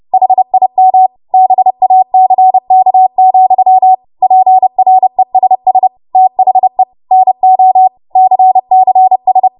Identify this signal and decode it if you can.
Morse code, heard as a keyed tone, about 30 wpm, 750 Hz. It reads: HIM BACK, PRESS THE NO CCS
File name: Morse / 30WPM